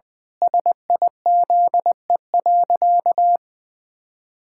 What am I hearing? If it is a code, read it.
Morse code, heard as a keyed tone, about 20 wpm, 700 Hz. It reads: SIZE.
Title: Pangram Challenge v2 - 20wpm